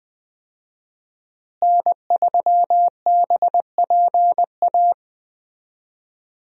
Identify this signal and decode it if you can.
Morse code, heard as a keyed tone, about 21 wpm, 700 Hz. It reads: N3BPA
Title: USA Call Signs - 5 Characters - Repeated 2x - 20wpm